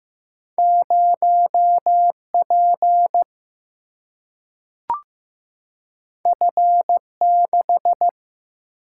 Morse code, heard as a keyed tone, about 15 wpm, 700 Hz. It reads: 0P F6
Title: Single-Letter Single-Number 15wpm